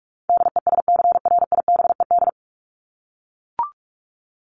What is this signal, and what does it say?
Morse code, heard as a keyed tone, about 44 wpm, 700 Hz. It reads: DESCRIBED
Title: ED Suffix - Top 100 Words with Base Words 45wpm